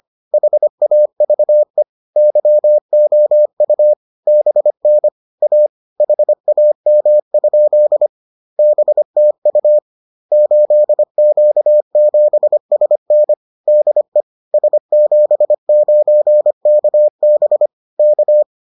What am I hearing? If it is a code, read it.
Morse code, heard as a keyed tone, about 25 wpm, 600 Hz. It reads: HAVE YOU BN A HAM? BTU 8Q7SN DE S79KB K